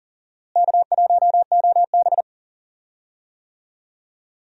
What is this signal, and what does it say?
Morse code, heard as a keyed tone, about 40 wpm, 700 Hz. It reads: K1OB
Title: USA Call Signs - 4 Characters - Repeated 2x - 40wpm